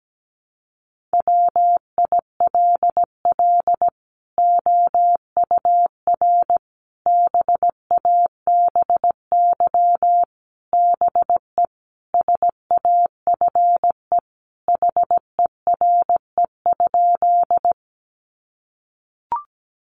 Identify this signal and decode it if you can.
Morse code, heard as a keyed tone, about 17 wpm, 700 Hz. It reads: WILL OUR BABY BE SAFE HERE?
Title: Sentences from Top 1300 Words 17wpm